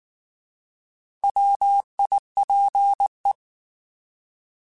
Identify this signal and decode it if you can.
Morse code, heard as a keyed tone, about 19 wpm, 800 Hz. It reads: WIPE
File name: Alphabetical / Alphabetical mp3s / Ws @18